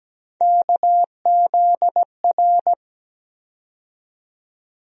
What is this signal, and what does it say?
Morse code, heard as a keyed tone, about 17 wpm, 700 Hz. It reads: KZR
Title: USA Call Sign Postfixes - 3 Letters 17wpm